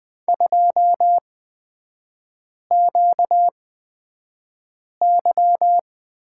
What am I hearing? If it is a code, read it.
Morse code, heard as a keyed tone, about 20 wpm, spaced as for 5 wpm, 700 Hz. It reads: 2QY
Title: 250 Lesson 40e - BK - Three Characters 20wpm